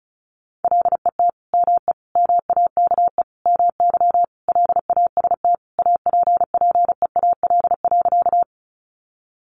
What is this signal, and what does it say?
Morse code, heard as a keyed tone, about 34 wpm, 700 Hz. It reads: LET ME MAKE MY LAST APPEAL.
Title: Sentences from Top 1300 Words 35wpm